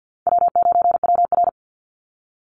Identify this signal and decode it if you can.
Morse code, heard as a keyed tone, about 50 wpm, 700 Hz. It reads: W9WR
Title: Easy Call Signs - 50wpm